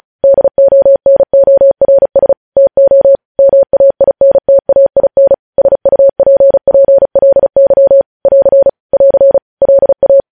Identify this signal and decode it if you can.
Morse code, heard as a keyed tone, about 36 wpm, 550 Hz. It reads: DONORS TO MAINTAIN SUPPLY <AR> <AR> LA
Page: news headlines in morse code at 35 wpm